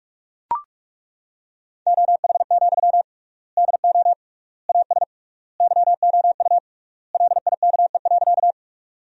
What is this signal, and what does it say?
Morse code, heard as a keyed tone, about 42 wpm, 700 Hz. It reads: OH, DO AS YOU LIKE.
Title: 3x Word-Spacing - Sentences from Top 500 Words - 45wpm